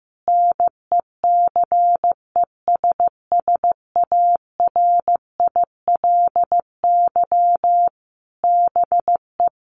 Morse code, heard as a keyed tone, about 15 wpm, 700 Hz. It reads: NECESSARILY BE
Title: Sentences from Top 1400 Words 15wpm